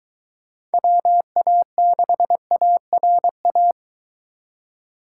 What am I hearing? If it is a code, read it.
Morse code, heard as a keyed tone, about 23 wpm, 700 Hz. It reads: WA6ARA